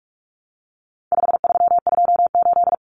Morse code, heard as a keyed tone, about 44 wpm, 700 Hz. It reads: 5328
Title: Numbers - 4-Digit 45wpm